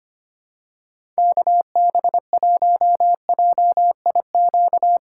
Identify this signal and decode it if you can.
Morse code, heard as a keyed tone, about 25 wpm, 700 Hz. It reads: KB1JIQ
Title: USA Call Signs - 8x Character Spacing - 25wpm